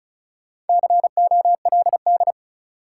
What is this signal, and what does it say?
Morse code, heard as a keyed tone, about 36 wpm, 700 Hz. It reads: COLD
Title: Sets of 5 Words - New Word Drill - 35wpm